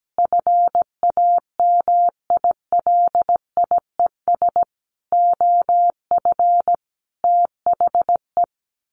Morse code, heard as a keyed tone, about 17 wpm, 700 Hz. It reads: FAMILIES OF THE
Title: A Sound Of Thunder 17wpm